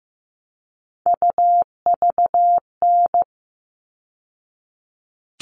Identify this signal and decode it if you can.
Morse code, heard as a keyed tone, about 15 wpm, 700 Hz. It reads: UVN
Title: All Cut Numbers - 3-Digit 15wpm